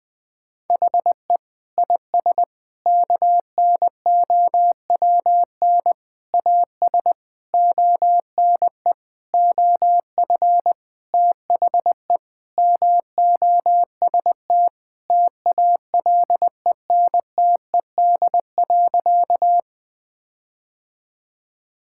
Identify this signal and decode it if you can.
Morse code, heard as a keyed tone, about 19 wpm, 700 Hz. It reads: HE IS KNOWN AS ONE OF THE MOST TALENTED.
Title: Sentences with 5-Word Phrases - Spoken Beginning - 20wpm